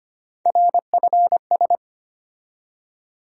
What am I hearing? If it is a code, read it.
Morse code, heard as a keyed tone, about 25 wpm, 700 Hz. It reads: RFS
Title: USA Call Sign Postfixes - 3 Letters 25wpm